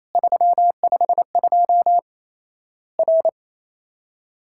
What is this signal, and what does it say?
Morse code, heard as a keyed tone, about 27 wpm, 700 Hz. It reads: 352 R
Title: Worked All Europe DX Contest - QTC Exchange - 28wpm